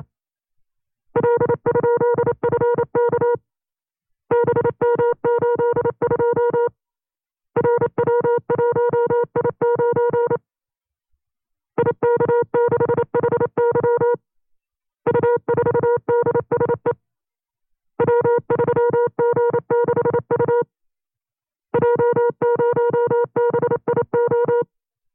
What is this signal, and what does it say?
Morse code, heard as a keyed tone, about 29 wpm, 480 Hz. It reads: L?FK BM82 RW1I9 IK6HY U4DSE W3G6U J0BIO
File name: FreeFiles / CW-practice / 25WPM-side2